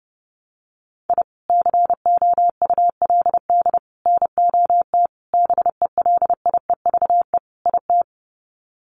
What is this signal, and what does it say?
Morse code, heard as a keyed tone, about 30 wpm, 700 Hz. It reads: I COULD NOT BELIEVE IT
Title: Sets of 5 Words Encore 30wpm